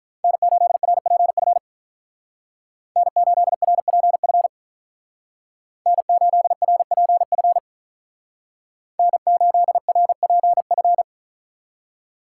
Morse code, heard as a keyed tone, about 45 wpm, 700 Hz. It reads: N8RPF N8RPF N8RPF N8RPF